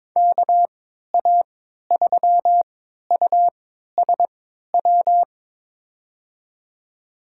Morse code, heard as a keyed tone, about 22 wpm, spaced as for 13 wpm, 700 Hz. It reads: KA3USW
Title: USA Call Signs - 6 Characters - 3x Character Spacing - 22wpm